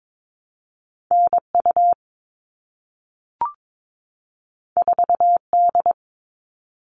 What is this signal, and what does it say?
Morse code, heard as a keyed tone, about 22 wpm, 700 Hz. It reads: NU 4D
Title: All Cut Numbers - 2-Digit 22wpm